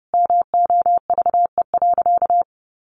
Morse code, heard as a keyed tone, about 30 wpm, 700 Hz. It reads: MOVE.
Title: A Sound Of Thunder 30wpm